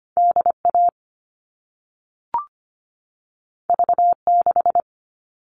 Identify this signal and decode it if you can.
Morse code, heard as a keyed tone, about 25 wpm, 700 Hz. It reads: DA V6
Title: All Cut Numbers - 2-Digit 25wpm